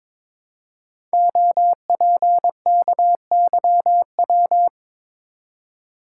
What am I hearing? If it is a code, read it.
Morse code, heard as a keyed tone, about 22 wpm, 700 Hz. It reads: OPKYW
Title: Code Groups - 5 Letter:Numbers - 22wpm